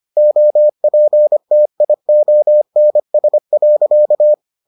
Morse code, heard as a keyed tone, about 25 wpm, 600 Hz. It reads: OPTIONS.